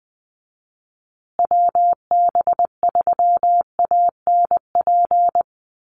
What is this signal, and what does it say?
Morse code, heard as a keyed tone, about 20 wpm, 700 Hz. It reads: WB3ANP
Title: USA Call Signs Repeated 2x 20wpm